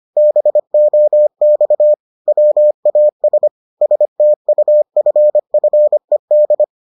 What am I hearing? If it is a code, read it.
Morse code, heard as a keyed tone, about 25 wpm, 600 Hz. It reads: BOX WAS STUFFED